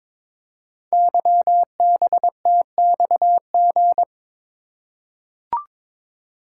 Morse code, heard as a keyed tone, about 22 wpm, 700 Hz. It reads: YBTXG
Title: Code Groups - 5 Letters - 22wpm